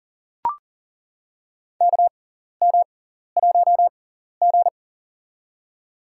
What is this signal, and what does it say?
Morse code, heard as a keyed tone, about 40 wpm, spaced as for 14 wpm, 700 Hz. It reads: KM1G
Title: USA Call Signs - 4 Characters - 6x Character Spacing - 40wpm